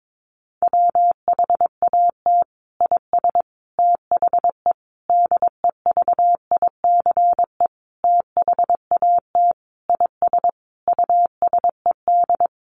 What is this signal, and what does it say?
Morse code, heard as a keyed tone, about 22 wpm, 700 Hz. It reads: WHAT IS THE DEVICE THAT IS USED
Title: Sentences from Top 1500 Words 22wpm